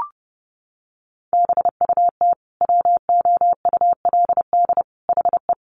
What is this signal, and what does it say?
Morse code, heard as a keyed tone, about 30 wpm, 700 Hz. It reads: BUT WOULD HE
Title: Sentences from Top 1400 Words 30wpm